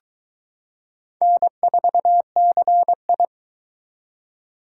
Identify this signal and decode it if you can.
Morse code, heard as a keyed tone, about 22 wpm, 700 Hz. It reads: N4CI